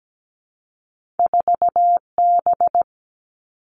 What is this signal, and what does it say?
Morse code, heard as a keyed tone, about 17 wpm, 700 Hz. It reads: 4B
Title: Two Challenging Letter-Number Combinations 17wpm